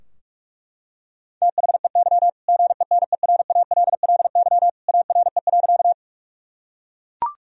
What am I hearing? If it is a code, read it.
Morse code, heard as a keyed tone, about 41 wpm, 700 Hz. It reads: THEY GENERALLY ARE.
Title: Sentences from Top 1000 Words - Review Part 4 - 45wpm